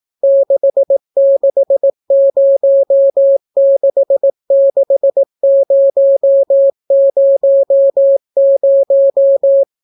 Morse code, heard as a keyed tone, about 18 wpm, 550 Hz. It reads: 66066000